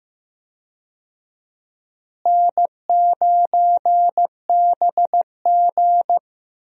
Morse code, heard as a keyed tone, about 15 wpm, 700 Hz. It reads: N9BG
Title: USA Call Signs - 4 Characters - Repeated 3x - 15wpm